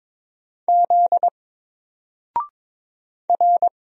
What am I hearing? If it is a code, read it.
Morse code, heard as a keyed tone, about 22 wpm, 700 Hz. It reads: Z R
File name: Single-Letter-Number-22wpm